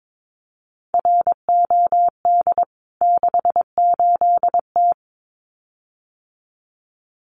Morse code, heard as a keyed tone, about 22 wpm, 700 Hz. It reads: ROD 68T
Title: CWT Exchange 22wpm